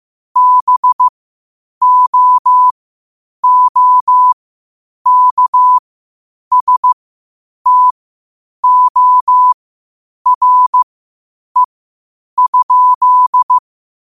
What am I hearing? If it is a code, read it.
Morse code, heard as a keyed tone, about 15 wpm, spaced as for 8 wpm, 1000 Hz. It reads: BOOKSTORE?
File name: qotdfiles / 8wpm